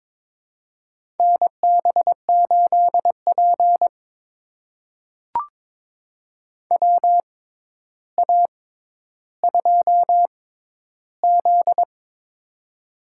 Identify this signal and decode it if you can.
Morse code, heard as a keyed tone, about 22 wpm, spaced as for 11 wpm, 700 Hz. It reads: NB8P WA2Z
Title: USA Call Signs - 4 Characters - 6x Character Spacing - 22wpm